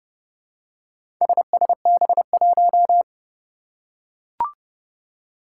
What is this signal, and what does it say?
Morse code, heard as a keyed tone, about 30 wpm, 690 Hz. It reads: SSB1
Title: Four Challenging Letter-Number Combinations 30wpm